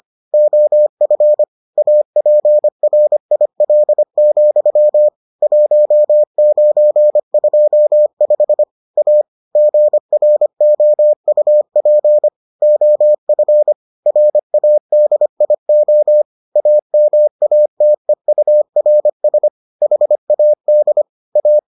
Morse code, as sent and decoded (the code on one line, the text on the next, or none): --- ..-. / .- .--. .-. .. .-.. --..-- / .---- ----. ..--- ..... / .- / --. .-. --- ..- .--. / --- ..-. / .-. .- -.. .. --- / .- -- .- - . ..- .-. ... / .... .- -.. / .-
OF APRIL, 1925 A GROUP OF RADIO AMATEURS HAD A